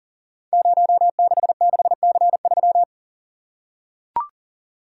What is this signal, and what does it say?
Morse code, heard as a keyed tone, about 42 wpm, 700 Hz. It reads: 066C3
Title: Code Groups - 5 Letter:Numbers - 40wpm